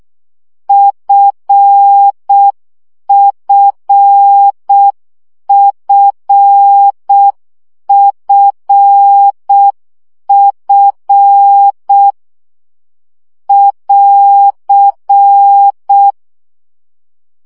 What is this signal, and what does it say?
Morse code, heard as a keyed tone, about 6 wpm, 800 Hz. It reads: FFFFF <AR>
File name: CW gyakorlas f 5 baud